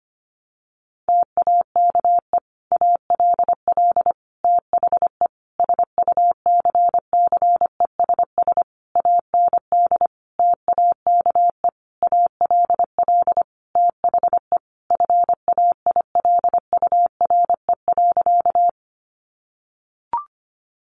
Morse code, heard as a keyed tone, about 25 wpm, 700 Hz. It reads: TAKE ALL THE SUCCESS AND TAKE ALL THE FAILURE.
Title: Sentences from Top 900 Words 25wpm